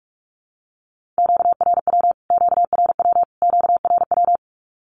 Morse code, heard as a keyed tone, about 44 wpm, 700 Hz. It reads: QRW QRW QRW
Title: Q-Codes Repeated 3x 45wpm